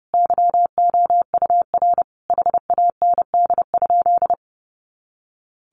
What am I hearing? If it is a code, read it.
Morse code, heard as a keyed tone, about 30 wpm, 700 Hz. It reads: YOUR HAND?
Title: Sentences from Top 200 Words 30wpm